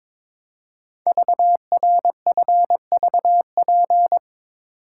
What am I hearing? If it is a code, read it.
Morse code, heard as a keyed tone, about 22 wpm, 700 Hz. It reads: VRFVP
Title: Code Groups - 5 Letters - 22wpm